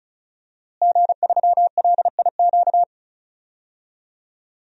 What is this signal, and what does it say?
Morse code, heard as a keyed tone, about 34 wpm, 700 Hz. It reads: G3LIQ
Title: DXCC-100 Call Signs - 5 Characters - Repeated 3x - 35wpm